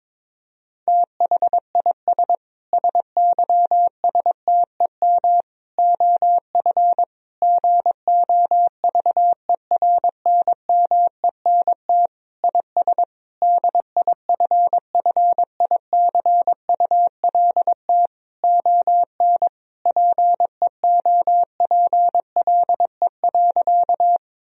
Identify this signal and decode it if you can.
Morse code, heard as a keyed tone, about 22 wpm, 700 Hz. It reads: THIS SYSTEM OF GOVERNMENT IS DIFFICULT ON PEOPLE.
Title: Sentences from Top 1000 Words - Review Part 3 - 22wpm